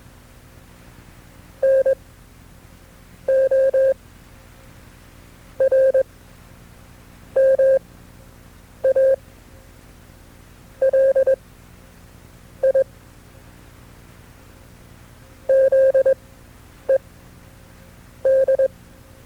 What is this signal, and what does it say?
Morse code, heard as a keyed tone, about 21 wpm, spaced as for 5 wpm, 530 Hz. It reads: NORMALIZED
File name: FreeFiles / CW-practice / KB6MT-tape2-side2